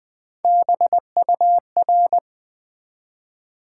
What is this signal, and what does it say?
Morse code, heard as a keyed tone, about 20 wpm, 700 Hz. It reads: BUR
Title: Build A Word - 1 Letter At A Time - 5-Letter Words - 20wpm